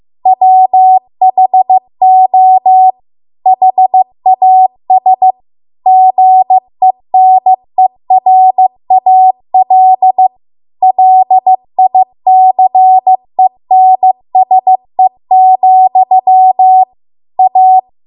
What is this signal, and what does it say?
Morse code, heard as a keyed tone, about 15 wpm, 750 Hz. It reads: WHO HAS GENERAL LICENSE, A